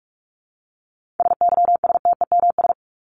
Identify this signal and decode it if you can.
Morse code, heard as a keyed tone, about 44 wpm, 700 Hz. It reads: SYSTEMS
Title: ES:S Suffix - Top 100 Words with Base Words 45wpm